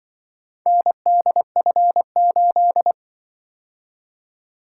Morse code, heard as a keyed tone, about 25 wpm, 700 Hz. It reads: NDF8
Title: Speed-Racing - Four Letter-Number Combinations - 20wpm